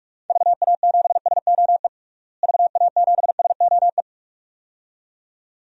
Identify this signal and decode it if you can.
Morse code, heard as a keyed tone, about 46 wpm, 700 Hz. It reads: VA7SOE VA7SOE
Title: DXCC-100 Call Signs - 6 Characters - Repeated 2x - 45wpm